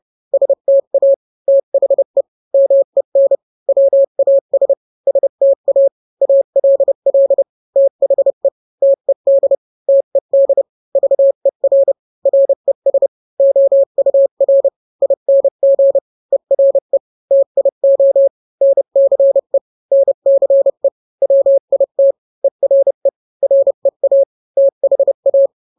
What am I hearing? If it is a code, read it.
Morse code, heard as a keyed tone, about 30 wpm, 550 Hz. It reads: STA THE MEN WAS STA ALL THE TED TED VER RES OUR ING ERE TIO NCE NCE WIT ERE REA THA